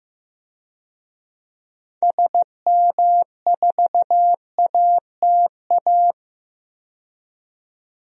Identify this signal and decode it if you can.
Morse code, heard as a keyed tone, about 15 wpm, 700 Hz. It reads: SM4ATA
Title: DXCC-100 Call Signs - 6 Characters - 15wpm